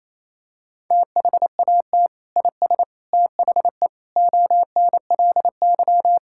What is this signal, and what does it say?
Morse code, heard as a keyed tone, about 28 wpm, 700 Hz. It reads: THAT IS THE ONLY
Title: Sentences from Top 1000 Words - Review Part 5 - 28wpm